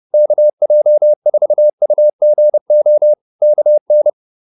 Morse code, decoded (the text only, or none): KJ4UGO KN